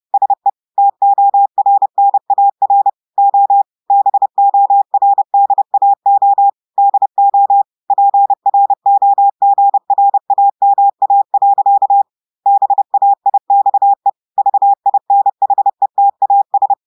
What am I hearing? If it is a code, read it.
Morse code, heard as a keyed tone, about 30 wpm, 800 Hz. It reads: SE TORNAR O BORDAO DO PROGRAMA. BAIXE VINHETAS
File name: portugais texte en clair 2 vitesse 30 mots minutes